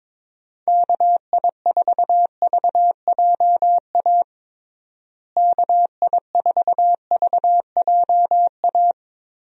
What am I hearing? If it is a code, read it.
Morse code, heard as a keyed tone, about 22 wpm, 700 Hz. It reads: KI4VJA KI4VJA
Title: USA Call Signs - 6 Characters - Repeated 2x - 22wpm